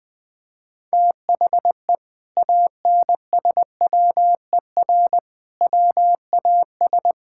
Morse code, heard as a keyed tone, about 20 wpm, 700 Hz. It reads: THE ANSWER WAS
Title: Sentences from Top 1000 Words - Review Part 6 - 20wpm